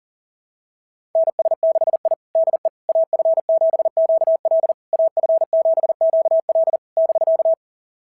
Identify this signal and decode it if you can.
Morse code, heard as a keyed tone, about 40 wpm, 690 Hz. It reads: NS6I DE AF7QL AF7QL <BK>
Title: SOTA QSO Practice 40wpm